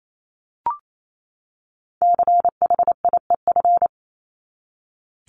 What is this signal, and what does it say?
Morse code, heard as a keyed tone, about 28 wpm, 700 Hz. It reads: CHIEF
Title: LY Suffix - Top 100 Words with Base Words 28wpm